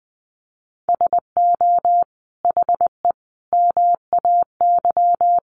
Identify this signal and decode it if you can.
Morse code, heard as a keyed tone, about 20 wpm, 700 Hz. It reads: SO HE MAY
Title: Sentences from Top 1500 Words 20wpm